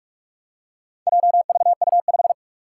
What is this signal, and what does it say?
Morse code, heard as a keyed tone, about 44 wpm, 700 Hz. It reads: JVU5
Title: Four Challenging Letter-Number Combinations 45wpm